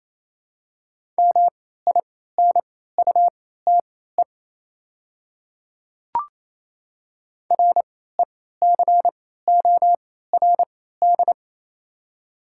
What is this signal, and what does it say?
Morse code, heard as a keyed tone, about 28 wpm, spaced as for 16 wpm, 700 Hz. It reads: MINUTE RECORD
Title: 3x Character Spacing - Top 500 Words - 28wpm